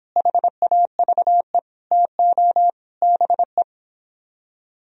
Morse code, heard as a keyed tone, about 29 wpm, 700 Hz. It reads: HAVE TO BE